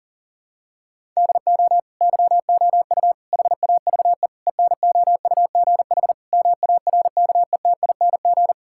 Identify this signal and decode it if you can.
Morse code, heard as a keyed tone, about 40 wpm, 700 Hz. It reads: DO YOU HAVE ENOUGH MARKETING
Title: Sentences from Top 1000 Words - Review Part 5 - 40wpm